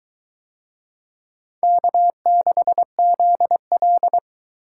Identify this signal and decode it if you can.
Morse code, heard as a keyed tone, about 22 wpm, 700 Hz. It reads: K6ZL